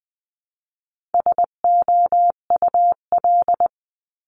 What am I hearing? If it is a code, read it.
Morse code, heard as a keyed tone, about 20 wpm, 700 Hz. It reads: SOUL
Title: 259 Lesson 42d - Words 18-20wpm